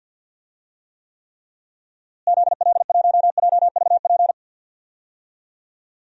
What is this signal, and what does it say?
Morse code, heard as a keyed tone, about 48 wpm, 700 Hz. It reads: ZL1JVP
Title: DXCC-100 Call Signs - 6 Characters - Repeated 3x - 50wpm